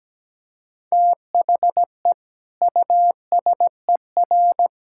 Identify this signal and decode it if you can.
Morse code, heard as a keyed tone, about 17 wpm, 700 Hz. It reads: THE USER
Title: Sentences from Top 1000 Words - Review Part 5 - 17wpm